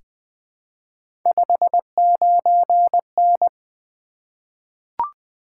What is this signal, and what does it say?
Morse code, heard as a keyed tone, about 20 wpm, 700 Hz. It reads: 59N
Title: Three Letter-Number Combinations - 20wpm